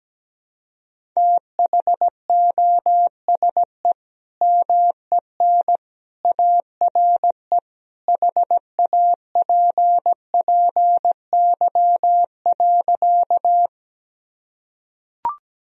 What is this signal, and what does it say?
Morse code, heard as a keyed tone, about 17 wpm, 700 Hz. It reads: THOSE MEN ARE HAPPY.
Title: Sentences from Top 1000 Words - Review Part 1 - 17wpm